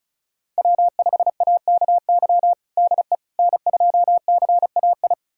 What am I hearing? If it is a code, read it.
Morse code, heard as a keyed tone, about 34 wpm, 700 Hz. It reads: W5AKY DE N2CAI
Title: USA Callsign DE Callsign - 5 Characters - 35wpm